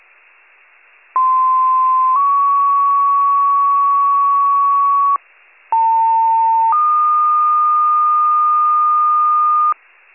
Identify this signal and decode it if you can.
Two-tone page, heard as a one-second tone, then a three-second tone, about 1040 then 1120 Hz, about 900 then 1220 Hz. It is heard from 1.2 to 9.7 s.
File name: wcalert